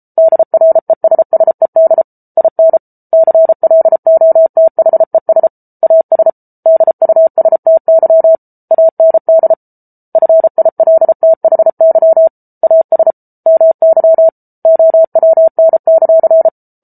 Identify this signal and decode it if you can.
Morse code, heard as a keyed tone, about 34 wpm, 650 Hz. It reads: DRESSED IN CLOTHES AS DUSTY AND FILTHY AS MY OWN;